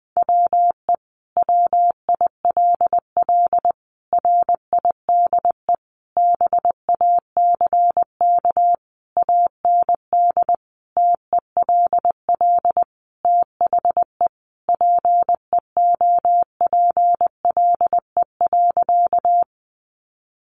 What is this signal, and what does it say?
Morse code, heard as a keyed tone, about 20 wpm, 700 Hz. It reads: WE WILL RIDE BACK AND TELL THE PEOPLE.
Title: Sentences from Top 1200 Words 20wpm